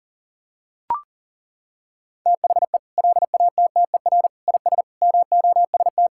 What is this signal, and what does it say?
Morse code, heard as a keyed tone, about 37 wpm, 700 Hz. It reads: THE LATTER IS MOST
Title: Sentences from Top 1600 Words 40wpm